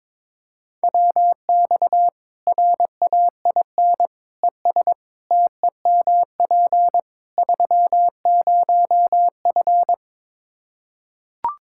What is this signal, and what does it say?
Morse code, heard as a keyed tone, about 22 wpm, 700 Hz. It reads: WX RAIN ES TEMP 30F
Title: K1USN SST QSO Practice 22wpm